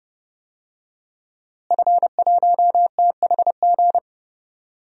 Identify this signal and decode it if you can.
Morse code, heard as a keyed tone, about 30 wpm, 700 Hz. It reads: F1THG
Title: DXCC-100 Call Signs - 5 Characters - Repeated 3x - 30wpm